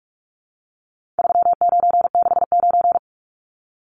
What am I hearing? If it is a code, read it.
Morse code, heard as a keyed tone, about 44 wpm, 700 Hz. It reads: 3969
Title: Numbers - 4-Digit 45wpm